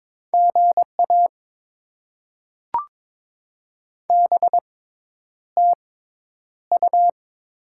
Morse code, heard as a keyed tone, about 22 wpm, spaced as for 10 wpm, 690 Hz. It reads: GA BTU
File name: Top 20 QSO Elements - 6x Character Spacing - 22wpm